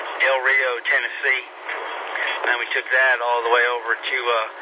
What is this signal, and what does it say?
Emergency Alert System (SAME) tones, heard as harsh buzzing bursts, with speech. It is heard from 1.1 to 4.2 s.